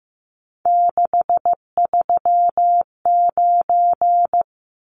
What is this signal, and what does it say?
Morse code, heard as a keyed tone, about 15 wpm, 700 Hz. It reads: 639
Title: Numbers - 3-Digit 15wpm